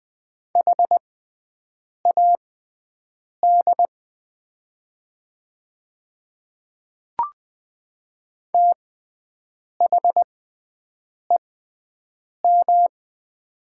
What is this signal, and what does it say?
Morse code, heard as a keyed tone, about 20 wpm, spaced as for 7 wpm, 700 Hz. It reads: HAD THEM
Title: 6x Character Spacing - Top 100 Words - 20wpm